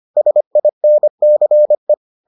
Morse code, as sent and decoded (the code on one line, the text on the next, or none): ... .. -. -.-. .
SINCE